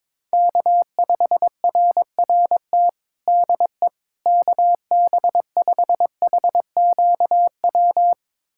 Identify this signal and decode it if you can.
Morse code, heard as a keyed tone, about 21 wpm, 700 Hz. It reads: K5RRT DE KB5HQW
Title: USA Callsign DE Callsign 22wpm